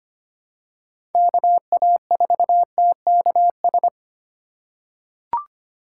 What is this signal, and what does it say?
Morse code, heard as a keyed tone, about 25 wpm, 700 Hz. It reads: KA4TKS
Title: USA Call Signs - 6 Characters - 4x Character Spacing - 25wpm